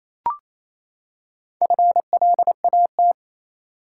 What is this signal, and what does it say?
Morse code, heard as a keyed tone, about 28 wpm, 700 Hz. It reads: FLAT
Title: Four Letter Words - 28wpm